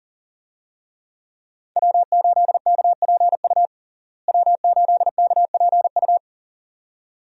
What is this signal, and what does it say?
Morse code, heard as a keyed tone, about 40 wpm, 700 Hz. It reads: W8KPU W8KPU
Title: USA Call Signs - 5 Characters - Repeated 2x - 40wpm